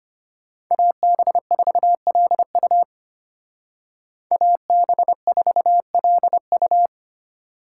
Morse code, heard as a keyed tone, about 33 wpm, 700 Hz. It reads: AB4LU AB4LU